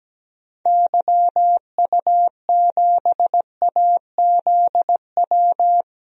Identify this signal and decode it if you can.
Morse code, heard as a keyed tone, about 17 wpm, 700 Hz. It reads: YU7AZW
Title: DXCC-100 Call Signs - 6 Characters - 17wpm